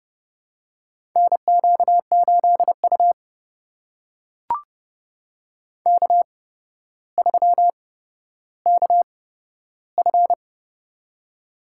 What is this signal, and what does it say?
Morse code, heard as a keyed tone, about 30 wpm, spaced as for 13 wpm, 700 Hz. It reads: NQ8U K3KF
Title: USA Call Signs - 4 Characters - 8x Character Spacing - 30wpm